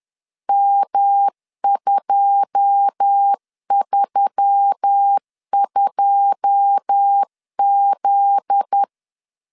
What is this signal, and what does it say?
Morse code, heard as a keyed tone, about 10 wpm, 800 Hz. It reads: M232Z
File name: LEZIONI / 9 2 10wpm 2736